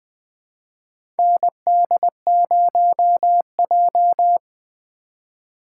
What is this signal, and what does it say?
Morse code, heard as a keyed tone, about 20 wpm, 700 Hz. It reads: ND0J
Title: USA Call Signs - 4 Characters - 2x Character Spacing - 20wpm